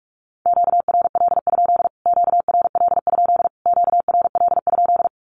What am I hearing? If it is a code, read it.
Morse code, heard as a keyed tone, about 44 wpm, 700 Hz. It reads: QRL? QRL? QRL?
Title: Q-Codes Repeated 3x 45wpm